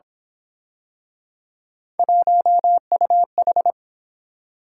Morse code, heard as a keyed tone, about 28 wpm, 700 Hz. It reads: 1UH